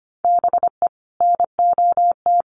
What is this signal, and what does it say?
Morse code, heard as a keyed tone, about 25 wpm, 700 Hz. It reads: BE NOT
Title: Sentences from Top 1500 Words 25wpm